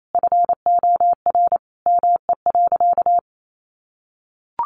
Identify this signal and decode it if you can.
Morse code, heard as a keyed tone, about 28 wpm, 700 Hz. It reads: FOR ME.
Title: Sentences from Top 400 Words 28wpm